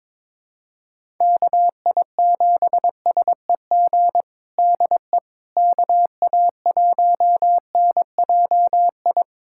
Morse code, heard as a keyed tone, about 22 wpm, 700 Hz. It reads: KI7SEG DE KA1NJI
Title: USA Callsign DE Callsign - 6 Characters - 22wpm